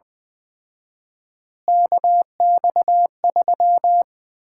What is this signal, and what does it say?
Morse code, heard as a keyed tone, about 20 wpm, 700 Hz. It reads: KX3
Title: Common QSO Phrases - 4x Word-Spacing - 20wpm